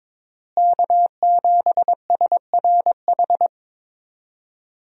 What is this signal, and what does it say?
Morse code, heard as a keyed tone, about 21 wpm, 700 Hz. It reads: K7SRH
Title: USA Call Signs - 5 Characters - Repeated 3x - 22wpm